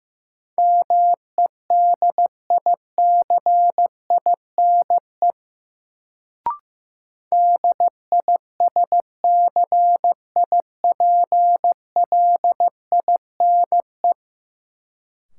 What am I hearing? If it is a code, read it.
Morse code, heard as a keyed tone, about 15 wpm, 700 Hz. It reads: MEDICINE DISCIPLINE
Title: Sentences from Top 900 Words - New Word Drill - 15wpm